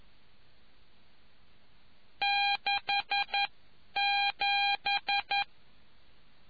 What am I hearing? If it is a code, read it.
Morse code, heard as a keyed tone, about 11 wpm, spaced as for 9 wpm, 800 Hz. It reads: 67